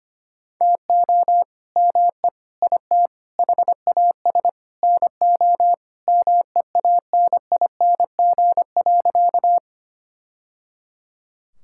Morse code, heard as a keyed tone, about 25 wpm, 700 Hz. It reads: TO ME IT HAS NO MEANING.
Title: Sentences from Top 1000 Words - Review Part 6 - 25wpm